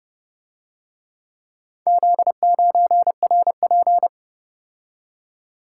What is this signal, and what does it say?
Morse code, heard as a keyed tone, about 29 wpm, 700 Hz. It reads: Z9RP